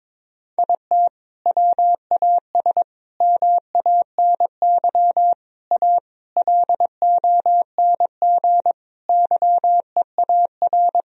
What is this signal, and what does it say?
Morse code, heard as a keyed tone, about 22 wpm, 700 Hz. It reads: IT WAS MANY A LONG YEAR
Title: Sentences from Top 1000 Words - Review Part 1 - 22wpm